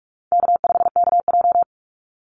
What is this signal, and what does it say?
Morse code, heard as a keyed tone, about 44 wpm, 700 Hz. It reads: K5KJ
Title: Call Signs - 45wpm